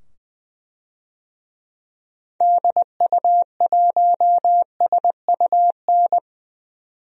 Morse code, heard as a keyed tone, about 20 wpm, 700 Hz. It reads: DU1SUN
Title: DXCC-100 Call Signs - 6 Characters - 20wpm